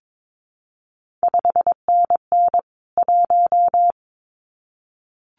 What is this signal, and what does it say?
Morse code, heard as a keyed tone, about 22 wpm, 700 Hz. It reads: 5NN 1
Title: CQ WW DX Contest Exchange Encore - 22wpm